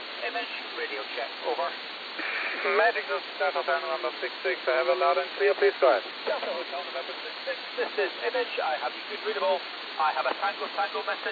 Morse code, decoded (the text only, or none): none